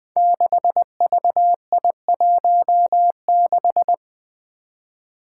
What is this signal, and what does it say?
Morse code, heard as a keyed tone, about 20 wpm, 700 Hz. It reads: 6VI16
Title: Five Challenging Letter-Number Combinations 20wpm